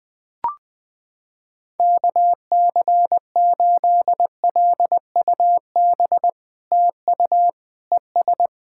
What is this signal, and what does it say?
Morse code, heard as a keyed tone, about 20 wpm, 700 Hz. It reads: KC8LUB TU ES
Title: POTA QSO Practice - 20wpm